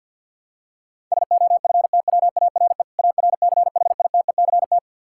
Morse code, heard as a keyed tone, about 52 wpm, 700 Hz. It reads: SOFTWARE ARCHITECT
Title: Practice QSO - Exchange 3 - More Info - 50wpm